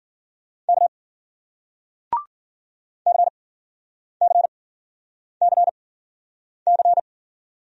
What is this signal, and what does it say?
Morse code, heard as a keyed tone, about 51 wpm, spaced as for 9 wpm, 700 Hz. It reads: K CCCC